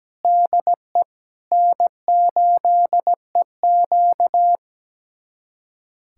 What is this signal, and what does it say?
Morse code, heard as a keyed tone, about 17 wpm, 700 Hz. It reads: DE N8EQ
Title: USA Callsign DE Callsign - 4 Characters - 17wpm